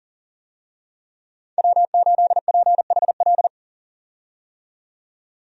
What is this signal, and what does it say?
Morse code, heard as a keyed tone, about 40 wpm, 700 Hz. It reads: W8PHL
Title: USA Call Signs - 5 Characters - 40wpm